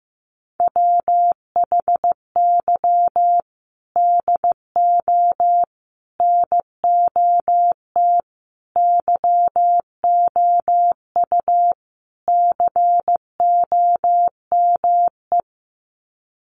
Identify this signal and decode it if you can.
Morse code, heard as a keyed tone, about 15 wpm, 700 Hz. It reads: WHY DO NOT YOU COME
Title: Sets of 5 Words Encore2 15wpm